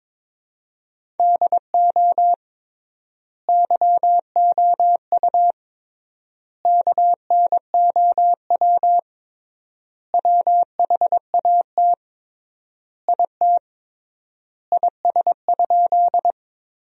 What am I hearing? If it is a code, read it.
Morse code, heard as a keyed tone, about 21 wpm, 700 Hz. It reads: DO YOU KNOW WHAT IT IS?
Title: 3x Word-Spacing - Sentences from Top 100 Words - 22wpm